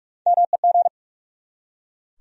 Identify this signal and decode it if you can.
Morse code, heard as a keyed tone, about 43 wpm, 700 Hz. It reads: MEG
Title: USA Common Names - Max 3 Letters - 45wpm